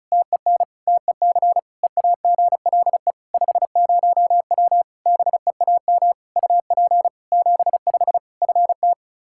Morse code, heard as a keyed tone, about 36 wpm, 700 Hz. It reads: TEN TEC EAGLE 50W BEAM UP 75 FT
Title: Practice QSO - Exchange 2 - Info - 35wpm